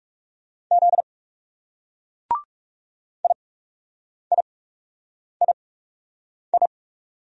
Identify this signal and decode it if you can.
Morse code, heard as a keyed tone, about 36 wpm, spaced as for 8 wpm, 700 Hz. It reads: Z IIII
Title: Speed-Racing Single Letters 30wpm